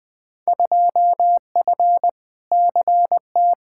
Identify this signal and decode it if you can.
Morse code, heard as a keyed tone, about 20 wpm, 700 Hz. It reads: 2F CT
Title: Field Day Exchanges Encore - 20wpm